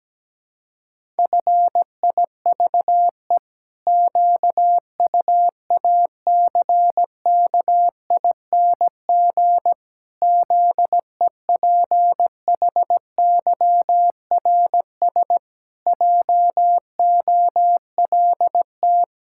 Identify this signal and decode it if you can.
Morse code, heard as a keyed tone, about 17 wpm, 700 Hz. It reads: FIVE QUACKING ZEPHYRS JOLT
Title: Pangram Challenge v2 - 17wpm